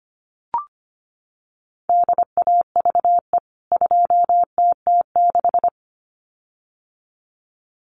Morse code, heard as a keyed tone, about 25 wpm, 700 Hz. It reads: DAVE 2TT6
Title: CWT Exchange 25wpm